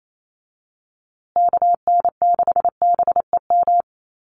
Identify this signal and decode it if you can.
Morse code, heard as a keyed tone, about 28 wpm, 690 Hz. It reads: KN6BEM
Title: USA Call Signs Repeated 3x 28wpm